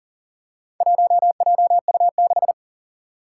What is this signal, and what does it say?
Morse code, heard as a keyed tone, about 40 wpm, 700 Hz. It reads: 1JU6
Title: Four Challenging Letter-Number Combinations 40wpm